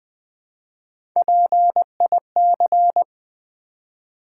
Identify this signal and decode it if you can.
Morse code, heard as a keyed tone, about 20 wpm, 700 Hz. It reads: PIC
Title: Build A Word - 1 Letter At A Time - 3 to 5-Letter Words - 20wpm